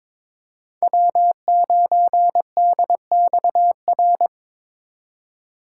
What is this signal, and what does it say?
Morse code, heard as a keyed tone, about 22 wpm, 700 Hz. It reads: W9DXR
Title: K1USN SST QSO Practice - 22wpm